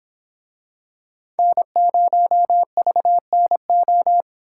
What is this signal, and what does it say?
Morse code, heard as a keyed tone, about 26 wpm, 700 Hz. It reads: N0VNO